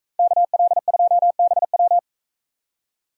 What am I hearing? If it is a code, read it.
Morse code, heard as a keyed tone, about 41 wpm, 700 Hz. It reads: KL2BW